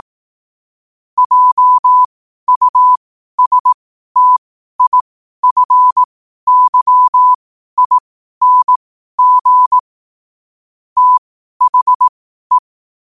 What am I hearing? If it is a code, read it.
Morse code, heard as a keyed tone, about 18 wpm, spaced as for 12 wpm, 1000 Hz. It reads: JUSTIFYING THE
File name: qotdfiles / 12wpm